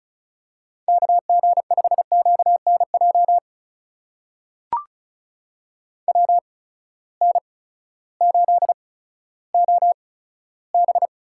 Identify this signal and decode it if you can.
Morse code, heard as a keyed tone, about 36 wpm, spaced as for 16 wpm, 700 Hz. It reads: KG5QNJ WN8OB
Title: USA Call Signs - 8x Character Spacing - 35wpm